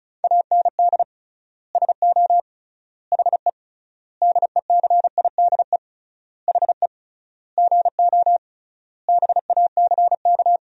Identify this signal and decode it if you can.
Morse code, heard as a keyed tone, about 36 wpm, 700 Hz. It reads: AND SO HE DECIDE HE GO BACK
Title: 3x Word-Spacing - Sentences from Top 500 Words - 35wpm